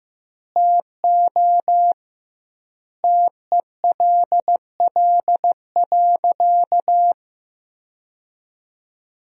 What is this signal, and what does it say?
Morse code, heard as a keyed tone, about 16 wpm, 700 Hz. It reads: TO TELL.
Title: 2x Word-Spacing - Sentences from Top 300 Words - 15wpm